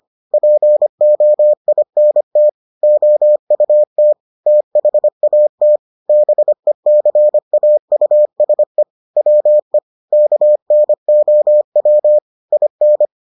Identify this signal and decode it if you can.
Morse code, heard as a keyed tone, about 25 wpm, 600 Hz. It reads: POINT OUT THAT BECAUSE WE KNOW IN